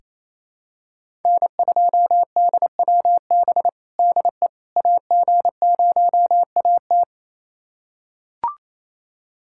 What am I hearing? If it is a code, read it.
Morse code, heard as a keyed tone, about 27 wpm, 700 Hz. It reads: N2DWB DE AG0AT
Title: USA Callsign DE Callsign - 5 Characters - 28wpm